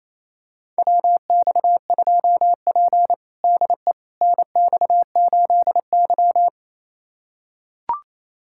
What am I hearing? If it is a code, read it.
Morse code, heard as a keyed tone, about 28 wpm, 700 Hz. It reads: WX2P DE NX8Y
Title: USA Callsign DE Callsign - 4 Characters - 28wpm